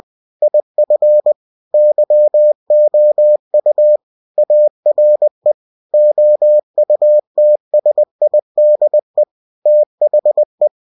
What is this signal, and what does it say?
Morse code, heard as a keyed tone, about 20 wpm, 600 Hz. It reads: IF YOU ARE OUTSIDE THE
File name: cw-books / winnie-the-pooh / winnie-20wpm-page-0114